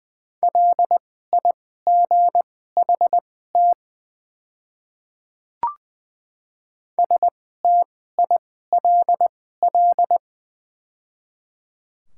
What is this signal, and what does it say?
Morse code, heard as a keyed tone, about 20 wpm, spaced as for 15 wpm, 690 Hz. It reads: LIGHT STILL
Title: 2x Character Spacing - Top 300 Words - 20wpm